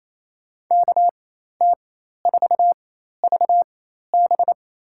Morse code, heard as a keyed tone, about 28 wpm, spaced as for 13 wpm, 700 Hz. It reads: KT4VB
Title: USA Call Signs - 5 Characters - 4x Character Spacing - 28wpm